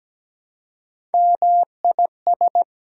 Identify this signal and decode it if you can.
Morse code, heard as a keyed tone, about 17 wpm, 700 Hz. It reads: MIS
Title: Prefixes and Suffixes 17wpm